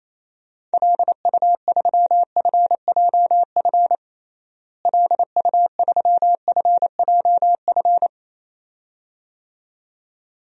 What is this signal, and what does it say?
Morse code, heard as a keyed tone, about 27 wpm, 700 Hz. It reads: LU3FJF LU3FJF
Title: DXCC-100 Call Signs - 6 Characters - Repeated 2x - 28wpm